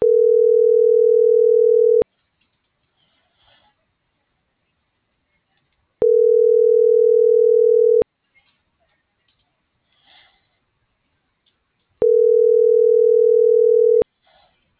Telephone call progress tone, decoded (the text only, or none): ringback tone